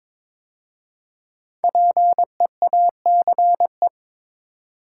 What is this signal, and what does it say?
Morse code, heard as a keyed tone, about 22 wpm, 700 Hz. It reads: PEACE
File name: practice / 22wpm / 5 Letter Words - First 3 Letters Given - 22wpm